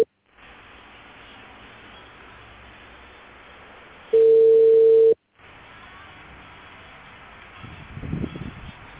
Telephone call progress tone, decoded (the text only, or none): ringback tone